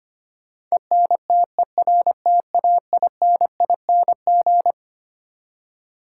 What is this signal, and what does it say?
Morse code, heard as a keyed tone, about 25 wpm, 700 Hz. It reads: ENTERTAINING
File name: Twelve Letter Words - 25wpm